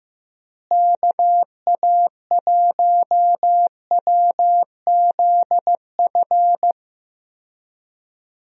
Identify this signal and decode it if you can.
Morse code, heard as a keyed tone, about 15 wpm, 700 Hz. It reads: KA1WZF
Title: USA Call Signs - 6 Characters - Repeated 3x - 15wpm